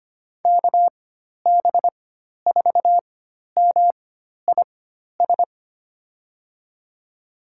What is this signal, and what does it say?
Morse code, heard as a keyed tone, about 25 wpm, spaced as for 12 wpm, 700 Hz. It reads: KB4MIS
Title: USA Call Signs - 6 Characters - 4x Character Spacing - 25wpm